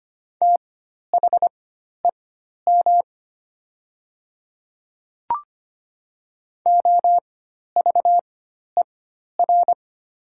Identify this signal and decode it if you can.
Morse code, heard as a keyed tone, about 25 wpm, spaced as for 12 wpm, 700 Hz. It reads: THEM OVER
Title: 4x Character Spacing - Top 100 Words - 25wpm